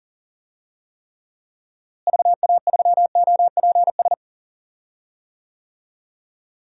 Morse code, heard as a keyed tone, about 40 wpm, 700 Hz. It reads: VA3OPS
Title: DXCC-100 Call Signs - 6 Characters - Repeated 2x - 40wpm